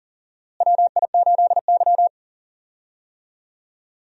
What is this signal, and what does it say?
Morse code, heard as a keyed tone, about 40 wpm, 700 Hz. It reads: WI8Y
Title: USA Call Signs - 4 Characters - Repeated 3x - 40wpm